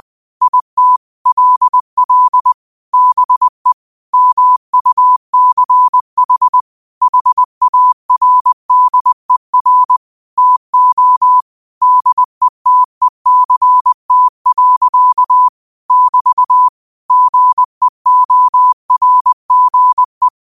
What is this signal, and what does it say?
Morse code, heard as a keyed tone, about 20 wpm, 1000 Hz. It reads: IT LL BE MUCH HARDER TO DETECT. <BT> GEORGE
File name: qotdfiles / 20wpm